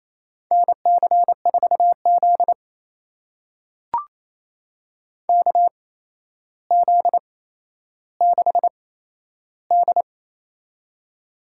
Morse code, heard as a keyed tone, about 28 wpm, spaced as for 12 wpm, 700 Hz. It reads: NC4Z KZ6D
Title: USA Call Signs - 4 Characters - 8x Character Spacing - 28wpm